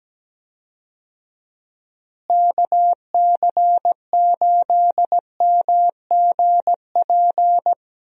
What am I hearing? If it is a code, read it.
Morse code, heard as a keyed tone, about 17 wpm, 700 Hz. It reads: KC8MGP
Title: USA Call Signs - 6 Characters - Repeated 2x - 17wpm